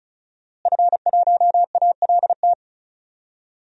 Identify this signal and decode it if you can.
Morse code, heard as a keyed tone, about 36 wpm, 700 Hz. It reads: F1ALT
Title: DXCC-100 Call Signs - 5 Characters - Repeated 3x - 35wpm